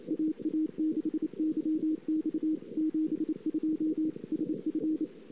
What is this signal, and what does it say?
Morse code, heard as a keyed tone, about 28 wpm, 310 Hz. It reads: AA6YX 72 SF
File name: CW / 2008-ss-cw-k5tr-op-n5rz -19:00